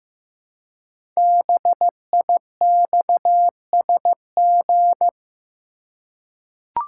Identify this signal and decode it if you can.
Morse code, heard as a keyed tone, about 15 wpm, 700 Hz. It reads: BIXSG
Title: Code Groups - 5 Letters - 15wpm